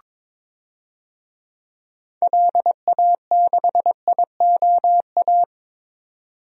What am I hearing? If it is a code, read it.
Morse code, heard as a keyed tone, about 22 wpm, 700 Hz. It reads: LA6IOA
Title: DXCC-100 Call Signs - 6 Characters - 22wpm